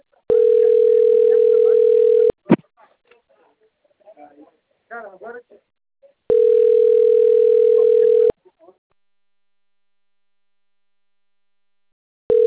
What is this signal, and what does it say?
Ringback tone (North America), 440 + 480 Hz, with speech.
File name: rg-1600-unknown- - - .78249